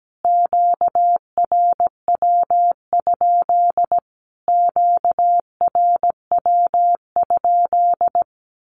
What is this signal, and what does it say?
Morse code, heard as a keyed tone, about 17 wpm, 700 Hz. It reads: QRW? QRW?
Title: Q-Codes Repeated 3x 17wpm